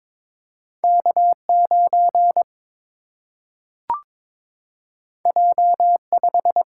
Morse code, heard as a keyed tone, about 22 wpm, 700 Hz. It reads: K 9 J 5
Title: Single-Letter Single-Number 22wpm